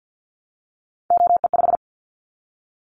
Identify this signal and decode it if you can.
Morse code, heard as a keyed tone, about 50 wpm, 700 Hz. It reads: OE5
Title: Callsign Prefixes for Top 25 DXCC - Repeated 3x - 50wpm